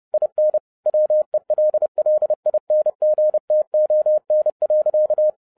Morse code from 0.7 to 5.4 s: WELLINGTON.